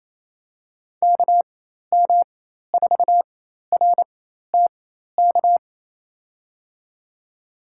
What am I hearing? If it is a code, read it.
Morse code, heard as a keyed tone, about 28 wpm, spaced as for 13 wpm, 700 Hz. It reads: KM4RTK
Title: USA Call Signs - 6 Characters - 4x Character Spacing - 28wpm